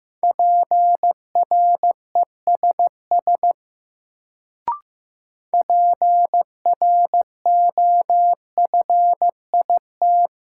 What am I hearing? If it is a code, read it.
Morse code, heard as a keyed tone, about 15 wpm, 700 Hz. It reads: PRESS PROFIT
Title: Sentences from Top 800 Words - New Word Drill - 15wpm